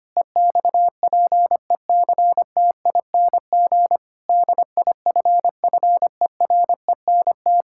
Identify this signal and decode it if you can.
Morse code, heard as a keyed tone, about 25 wpm, 700 Hz. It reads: EXPECTING DIFFERENT
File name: practice / 25wpm / Sentences with 9-Word Phrases - 25wpm